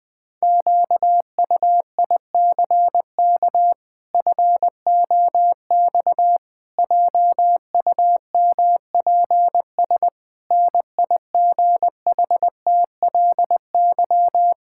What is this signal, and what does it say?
Morse code, heard as a keyed tone, about 20 wpm, 700 Hz. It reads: QUICK FOX JUMPS NIGHTLY
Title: Pangram Challenge v2 - 20wpm